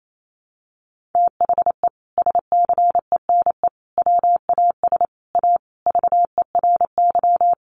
Morse code, heard as a keyed tone, about 28 wpm, 700 Hz. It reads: THE SCENE WAS A VERY
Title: Sentences from Top 1100 Words 28wpm